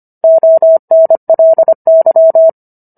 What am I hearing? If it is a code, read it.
Morse code, heard as a keyed tone, about 25 wpm, 650 Hz. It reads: ONLY